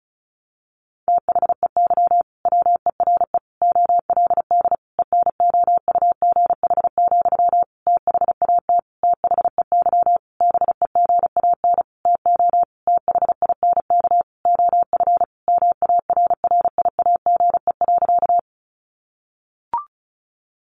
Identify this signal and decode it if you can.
Morse code, heard as a keyed tone, about 34 wpm, 700 Hz. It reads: THEY WERE OLD ENOUGH, THAT THEY BEGAN TO THINK OF MARRIAGE.
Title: Sentences from Top 800 Words 35wpm